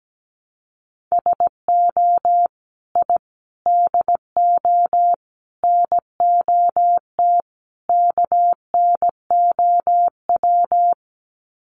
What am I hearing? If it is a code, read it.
Morse code, heard as a keyed tone, about 17 wpm, 700 Hz. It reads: SO I DO NOT KNOW
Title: Sets of 5 Words Encore 17wpm